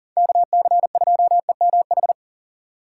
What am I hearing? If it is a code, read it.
Morse code, heard as a keyed tone, about 40 wpm, 700 Hz. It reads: KC2EMH
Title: USA Call Signs - 6 Characters - 4x Character Spacing - 40wpm